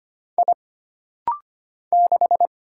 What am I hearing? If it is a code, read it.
Morse code, heard as a keyed tone, about 25 wpm, 700 Hz. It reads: I 6
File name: Single Character (Letter:Number:Punctuation:Prosign) - Rapid-Fire - 25wpm